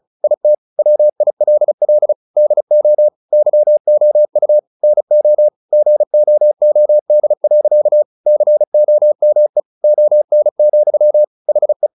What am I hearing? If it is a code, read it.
Morse code, heard as a keyed tone, about 34 wpm, 600 Hz. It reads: IT WILL DO YOU NO GOOD. COME ON, HE
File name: the old man and the sea / the old man and the sea 1 35wpm